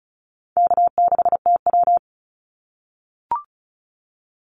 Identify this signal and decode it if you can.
Morse code, heard as a keyed tone, about 36 wpm, 700 Hz. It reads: K6TW
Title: NA SOTA Activator Callsigns 35wpm